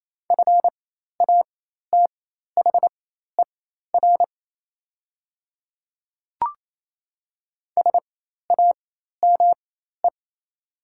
Morse code, heard as a keyed tone, about 28 wpm, spaced as for 13 wpm, 690 Hz. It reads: FATHER SAME
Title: 4x Character Spacing - Top 200 Words - 28wpm